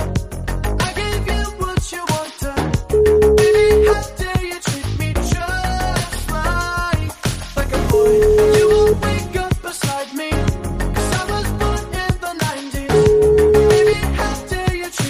Ringback tone (Europe, 425 Hz), with noise.